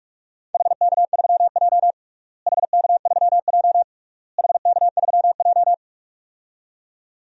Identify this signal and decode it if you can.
Morse code, heard as a keyed tone, about 42 wpm, 700 Hz. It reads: HK3J HK3J HK3J
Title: DXCC-100 Call Signs - 4 Characters - Repeated 3x - 45wpm